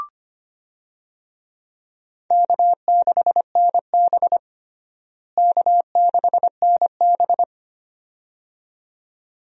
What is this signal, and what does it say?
Morse code, heard as a keyed tone, about 25 wpm, 700 Hz. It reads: K6NB K6NB
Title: USA Call Signs - 4 Characters - Repeated 2x - 25wpm